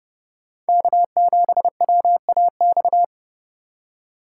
Morse code, heard as a keyed tone, about 28 wpm, 700 Hz. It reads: K7WAX